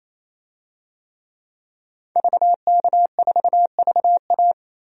Morse code, heard as a keyed tone, about 28 wpm, 700 Hz. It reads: VK4VA
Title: DXCC-100 Call Signs - 5 Characters - Repeated 3x - 28wpm